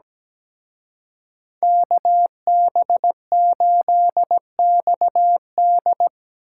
Morse code, heard as a keyed tone, about 19 wpm, 700 Hz. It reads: KB8XD
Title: Speed Racing - USA Call Signs 17wpm